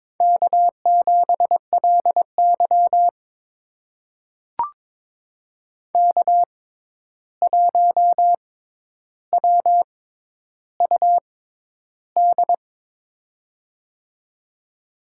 Morse code, heard as a keyed tone, about 22 wpm, spaced as for 11 wpm, 700 Hz. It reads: K7LY K1WUD
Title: USA Call Signs - 6x Character Spacing - 22wpm